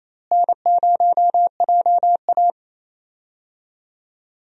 Morse code, heard as a keyed tone, about 28 wpm, 700 Hz. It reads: N0JA
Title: USA Call Signs - 4 Characters - Repeated 2x - 28wpm